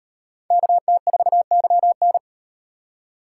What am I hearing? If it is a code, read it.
Morse code, heard as a keyed tone, about 36 wpm, 700 Hz. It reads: KT4YN